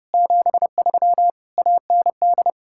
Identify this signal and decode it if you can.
Morse code, heard as a keyed tone, about 39 wpm, 700 Hz. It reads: 73 AND